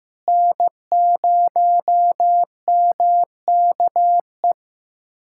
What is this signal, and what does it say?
Morse code, heard as a keyed tone, about 15 wpm, 700 Hz. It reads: N0MKE
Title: USA Callsign DE Callsign - 5 Characters - 15wpm